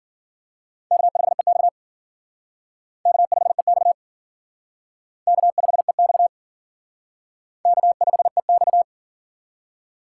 Morse code, heard as a keyed tone, about 50 wpm, 700 Hz. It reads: K5EX K5EX K5EX K5EX
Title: Speed Racing - USA 1x2 Call Signs 40wpm